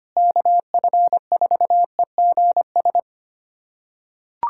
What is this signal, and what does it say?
Morse code, heard as a keyed tone, about 25 wpm, 700 Hz. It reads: KF4EGS
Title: USA Callsign DE Callsign - 6 Characters - 25wpm